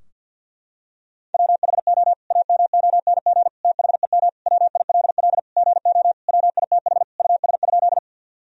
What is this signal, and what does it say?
Morse code, heard as a keyed tone, about 52 wpm, 700 Hz. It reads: WHO AMONG THEM WILL GO WITH US?
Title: Sentences from Top 1000 Words - Review Part 3 - 50wpm